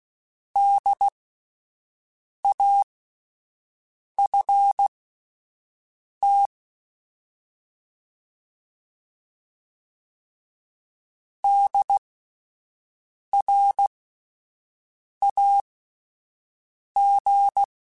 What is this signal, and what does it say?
Morse code, heard as a keyed tone, about 16 wpm, spaced as for 5 wpm, 800 Hz. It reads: DAFT DRAG